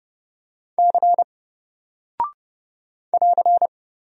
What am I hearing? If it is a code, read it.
Morse code, heard as a keyed tone, about 30 wpm, 700 Hz. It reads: C <AR>
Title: Single Character (Letter:Number:Punctuation:Prosign) Alt 30wpm